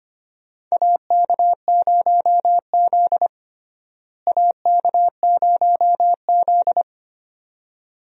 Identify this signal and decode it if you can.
Morse code, heard as a keyed tone, about 25 wpm, 700 Hz. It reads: AK0Z AK0Z
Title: USA Call Signs - 4 Characters - Repeated 2x - 25wpm